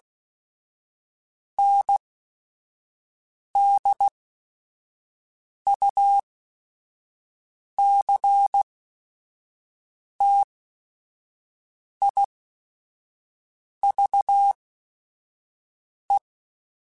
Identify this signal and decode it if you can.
Morse code, heard as a keyed tone, about 16 wpm, spaced as for 5 wpm, 800 Hz. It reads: NDUCTIVE
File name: Technical Subjects / Technical Subjects-mp3s / XL @5